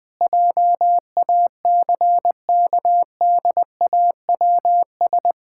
Morse code, heard as a keyed tone, about 20 wpm, 700 Hz. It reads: JACKDAWS
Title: Pangram Challenge v2 - 20wpm